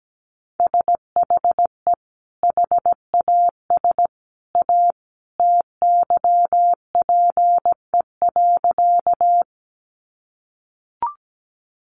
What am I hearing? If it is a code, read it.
Morse code, heard as a keyed tone, about 17 wpm, 700 Hz. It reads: SHE HAS A TYPE.
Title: Sentences from Top 600 Words 17wpm